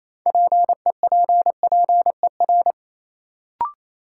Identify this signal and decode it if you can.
Morse code, heard as a keyed tone, about 27 wpm, 700 Hz. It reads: PEPPER
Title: Sets of 5 Words - New Word Drill - 28wpm